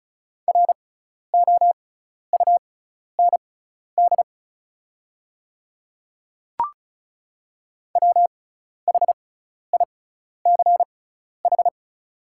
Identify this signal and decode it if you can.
Morse code, heard as a keyed tone, about 34 wpm, spaced as for 12 wpm, 700 Hz. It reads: ROUND WHICH
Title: 6x Character Spacing - Top 200 Words - 35wpm